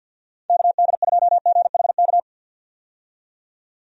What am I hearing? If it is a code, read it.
Morse code, heard as a keyed tone, about 47 wpm, 700 Hz. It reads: KD2GHK
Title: USA Call Signs - 6 Characters - 50wpm